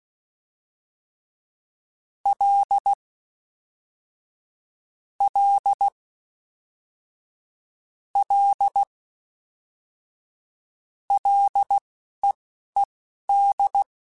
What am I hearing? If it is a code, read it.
Morse code, heard as a keyed tone, about 16 wpm, spaced as for 11 wpm, 800 Hz. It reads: L L L LEED
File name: F and L / F and L mp3 / F and L Prefixes 3x2 Twice @10